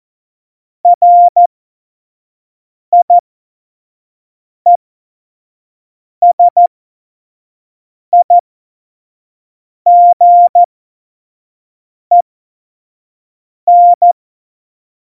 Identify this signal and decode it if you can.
Morse code, heard as a keyed tone, about 14 wpm, spaced as for 5 wpm, 700 Hz. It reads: RIESIGEN
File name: Morsekurs / Witze / Witz 15. Januar 2019, 25 F70